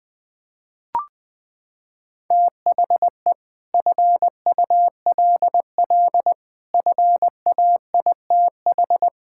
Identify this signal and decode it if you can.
Morse code, heard as a keyed tone, about 21 wpm, 700 Hz. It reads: THE FULL FAITH
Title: Sets of 9 Words - 20wpm